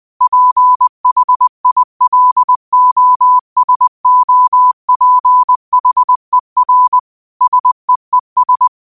Morse code, heard as a keyed tone, about 20 wpm, 1000 Hz. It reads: PHILOSOPHER SEES
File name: qotdfiles / 20wpm